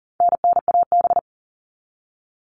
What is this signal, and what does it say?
Morse code, heard as a keyed tone, about 40 wpm, 700 Hz. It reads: NNAB
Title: All Cut Numbers - 4-Digit 40wpm